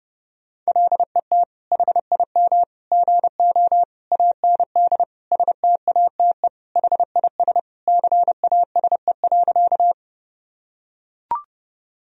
Morse code, heard as a keyed tone, about 30 wpm, 700 Hz. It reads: LET HIM GO AND STATE HIS CASE.
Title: Sentences from Top 1000 Words - Review Part 2 - 30wpm